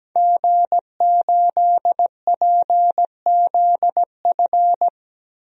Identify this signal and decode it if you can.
Morse code, heard as a keyed tone, about 17 wpm, 700 Hz. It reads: G8PZF
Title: DXCC-100 Call Signs - 5 Characters - 17wpm